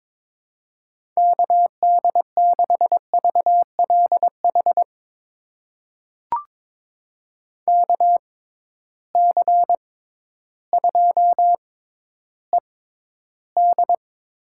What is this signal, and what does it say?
Morse code, heard as a keyed tone, about 22 wpm, spaced as for 12 wpm, 700 Hz. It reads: KD6VLH KC2ED
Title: USA Call Signs - 6 Characters - 6x Character Spacing - 22wpm